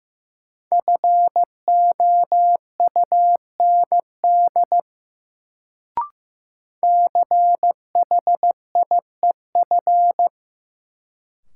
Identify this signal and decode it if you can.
Morse code, heard as a keyed tone, about 15 wpm, 700 Hz. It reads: FOUND CHIEF
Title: Sets of 5 Words Encore - New Word Drill - 15wpm